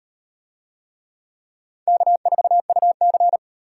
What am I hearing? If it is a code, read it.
Morse code, heard as a keyed tone, about 36 wpm, 700 Hz. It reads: K4UC